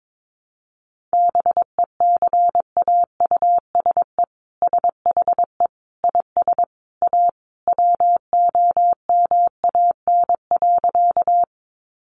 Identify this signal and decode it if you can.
Morse code, heard as a keyed tone, about 22 wpm, 700 Hz. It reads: BECAUSE SHE IS A WOMAN.
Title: Sentences from Top 1100 Words 22wpm